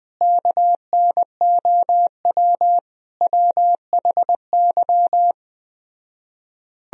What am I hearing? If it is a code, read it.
Morse code, heard as a keyed tone, about 20 wpm, 700 Hz. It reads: KNOW WHY
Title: Speed-Racing - Sets of 5 Words 17wpm